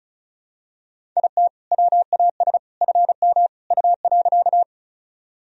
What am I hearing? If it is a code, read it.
Morse code, heard as a keyed tone, about 36 wpm, 700 Hz. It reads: IT WAS FM U.
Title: Sentences from Top 100 Words and Abbreviations 35wpm